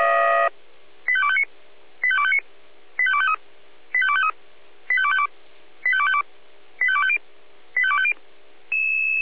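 Five-tone selective call (ZVEI), heard as quick runs of five tones, tones of about 70 ms. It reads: 85259 85259 85253 85253 85252 85252 85250 85250
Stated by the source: ZVEI five-tone selective call received on SDR.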